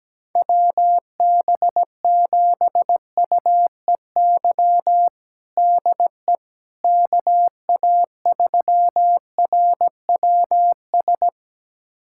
Morse code, heard as a keyed tone, about 17 wpm, 700 Hz. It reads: WB7UEY DE KA3RWS
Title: USA Callsign DE Callsign - 6 Characters - 17wpm